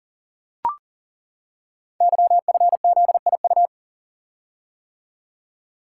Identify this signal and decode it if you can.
Morse code, heard as a keyed tone, about 38 wpm, 700 Hz. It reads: YFZIU
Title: Code Groups - 5 Letters - 40wpm